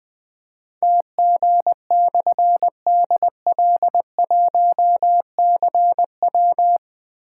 Morse code, heard as a keyed tone, about 20 wpm, 700 Hz. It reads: TG/DL1CW
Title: Difficult Call Signs 20wpm